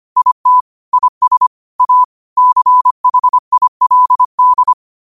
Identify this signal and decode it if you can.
Morse code, heard as a keyed tone, about 25 wpm, 1000 Hz. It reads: IT IS A CHILD